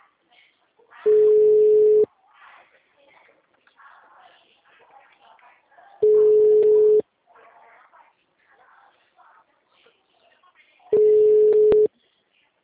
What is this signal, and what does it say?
Ringback tone (Europe, 425 Hz), with speech.